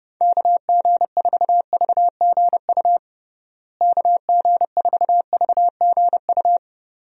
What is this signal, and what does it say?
Morse code, heard as a keyed tone, about 30 wpm, 700 Hz. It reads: KG4VGU KG4VGU
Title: USA Call Signs - 6 Characters - Repeated 2x - 30wpm